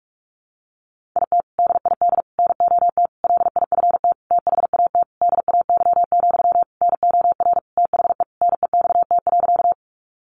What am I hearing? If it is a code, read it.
Morse code, heard as a keyed tone, about 44 wpm, 700 Hz. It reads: IT DID NOT LIFT THAT DAY, NOR THE NEXT.
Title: Sentences from Top 1500 Words 45wpm